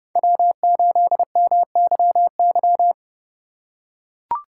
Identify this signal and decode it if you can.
Morse code, heard as a keyed tone, about 30 wpm, 700 Hz. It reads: W8MYY